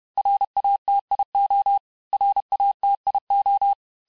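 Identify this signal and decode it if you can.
Morse code, heard as a keyed tone, about 31 wpm, 800 Hz. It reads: RATIO RATIO
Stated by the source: Morse lesson of words, each word repeated 4 times, at 29 words per minute.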